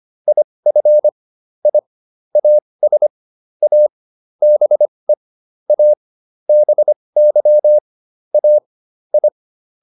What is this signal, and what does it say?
Morse code, heard as a keyed tone, about 25 wpm, spaced as for 20 wpm, 600 Hz. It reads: IF I AS A BE A BY A I
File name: WD101a 20